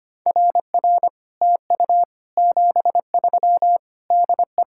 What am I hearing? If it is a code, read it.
Morse code, heard as a keyed tone, about 30 wpm, 700 Hz. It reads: RR TU 73 DE
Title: Speed-Racing Common QSO Phrases Encore - 25wpm